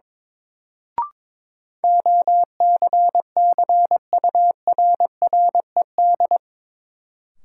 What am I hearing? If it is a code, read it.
Morse code, heard as a keyed tone, about 22 wpm, 700 Hz. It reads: OCCURRED
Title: Sets of 5 Words Encore - New Word Drill - 22wpm